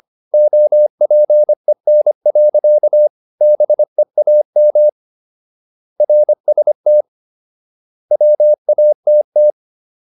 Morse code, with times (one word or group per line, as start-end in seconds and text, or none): 0.3-3.1 OPEN.
3.4-4.9 BEAM
6.0-7.0 RST
8.1-9.5 WATT